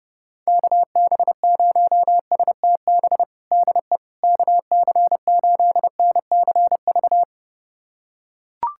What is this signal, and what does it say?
Morse code, heard as a keyed tone, about 30 wpm, 700 Hz. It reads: KB0STB DE KC8NCV
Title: USA Callsign DE Callsign - 6 Characters - 30wpm